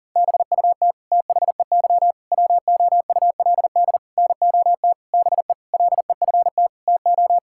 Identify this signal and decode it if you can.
Morse code, heard as a keyed tone, about 40 wpm, 700 Hz. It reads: BUT THEY WOULD NOT BE LEFT TO
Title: Sentences from Top 1000 Words - Review Part 5 - 40wpm